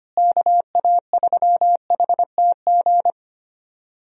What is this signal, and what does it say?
Morse code, heard as a keyed tone, about 25 wpm, 700 Hz. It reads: KA3HTG
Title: USA Call Signs - 6 Characters - Repeated 3x - 25wpm